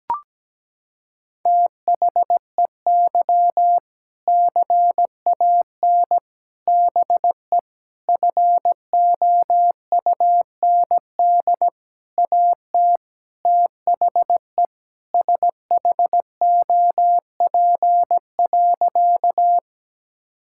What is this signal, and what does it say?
Morse code, heard as a keyed tone, about 17 wpm, 700 Hz. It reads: THEY CAN BE FOUND AT THE SHOP.
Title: Sentences from Top 1600 Words 17wpm